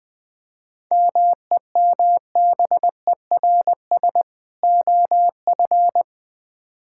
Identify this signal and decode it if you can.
Morse code, heard as a keyed tone, about 20 wpm, 700 Hz. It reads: MEMBERS OF
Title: Build A Phrase - 3-Word Phrases - 20wpm